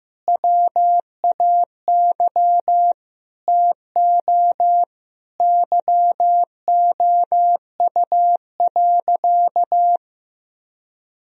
Morse code, heard as a keyed tone, about 15 wpm, 700 Hz. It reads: WAY TO YOU.
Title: Sentences from Top 1000 Words - Review Part 1 - 15wpm